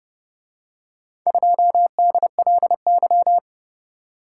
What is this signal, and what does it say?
Morse code, heard as a keyed tone, about 30 wpm, 700 Hz. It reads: 2DLY